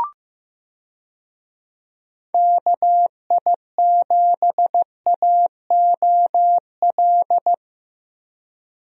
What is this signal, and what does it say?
Morse code, heard as a keyed tone, about 15 wpm, 700 Hz. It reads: KI7AOL
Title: USA Call Signs - 6 Characters - Repeated 3x - 15wpm